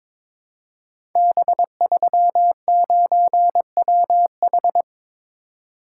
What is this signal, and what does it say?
Morse code, heard as a keyed tone, about 22 wpm, 700 Hz. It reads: B39WH
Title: Code Groups - 5 Letter:Numbers - 22wpm